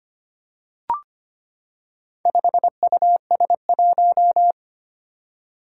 Morse code, heard as a keyed tone, about 25 wpm, 700 Hz. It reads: 5US1
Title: Four Challenging Letter-Number Combinations 25wpm